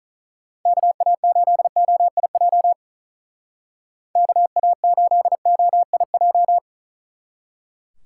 Morse code, heard as a keyed tone, about 44 wpm, 700 Hz. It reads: KA8OIJ KA8OIJ